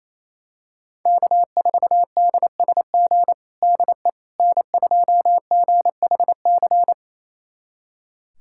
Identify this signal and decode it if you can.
Morse code, heard as a keyed tone, about 28 wpm, 700 Hz. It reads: K4DSG DE N2GHC
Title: USA Callsign DE Callsign - 5 Characters - 28wpm